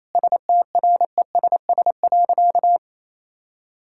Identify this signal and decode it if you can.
Morse code, heard as a keyed tone, about 28 wpm, 700 Hz. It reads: STRESS.
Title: Sentences from Top 1000 Words - Review Part 6 - 28wpm